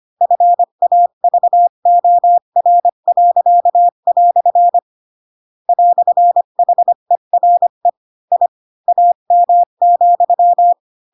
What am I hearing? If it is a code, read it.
Morse code, heard as a keyed tone, about 25 wpm, 700 Hz. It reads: FAVOR." "HERE I AM,